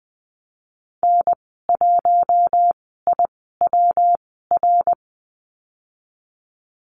Morse code, heard as a keyed tone, about 20 wpm, spaced as for 15 wpm, 700 Hz. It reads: N1IWR
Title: 167 Lesson 27g - Number 7 - Call Signs 20wpm